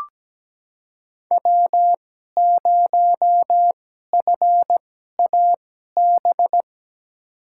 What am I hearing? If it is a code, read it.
Morse code, heard as a keyed tone, about 17 wpm, spaced as for 12 wpm, 700 Hz. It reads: W0FAB
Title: USA Call Signs - 5 Characters - 2x Character Spacing - 17wpm